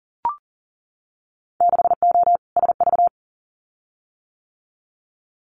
Morse code, heard as a keyed tone, about 38 wpm, 700 Hz. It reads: 6O SV
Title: Winter Field Day Exchanges Encore 40wpm